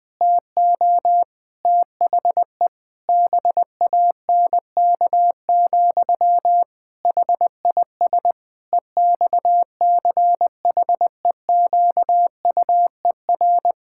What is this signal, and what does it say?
Morse code, heard as a keyed tone, about 20 wpm, 700 Hz. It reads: TO THE BANK, HIS EXCHEQUER
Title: Pangram Challenge v2 - 20wpm